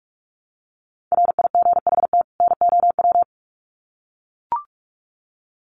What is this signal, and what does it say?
Morse code, heard as a keyed tone, about 44 wpm, 700 Hz. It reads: RIGHT NOW
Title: Sets of 2 Words 45wpm